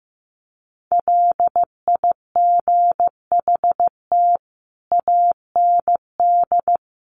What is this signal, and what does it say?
Morse code, heard as a keyed tone, about 15 wpm, 700 Hz. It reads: LIGHT AND
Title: Sentences from Top 1300 Words 15wpm